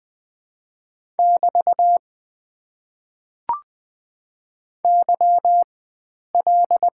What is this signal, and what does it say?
Morse code, heard as a keyed tone, about 20 wpm, spaced as for 9 wpm, 700 Hz. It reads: <BT> YL
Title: Top 90 QSO Elements - 4x Character Spacing - 20wpm